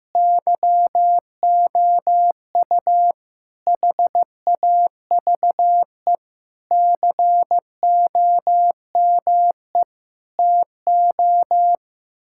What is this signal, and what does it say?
Morse code, heard as a keyed tone, about 15 wpm, 700 Hz. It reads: YOU HAVE COME TO
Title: Sentences from Top 1600 Words 15wpm